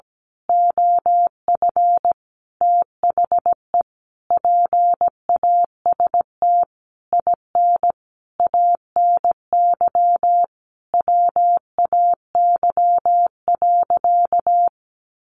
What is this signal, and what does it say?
Morse code, heard as a keyed tone, about 17 wpm, 700 Hz. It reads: OF THE PAST IN ANY WAY.
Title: A Sound Of Thunder 17wpm